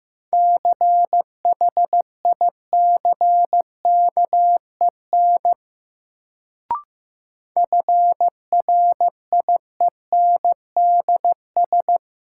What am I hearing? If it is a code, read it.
Morse code, heard as a keyed tone, about 15 wpm, 700 Hz. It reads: CHICKEN FRIENDS
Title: Sentences from Top 1000 Words - Review Part 5 - New Word Drill - 15wpm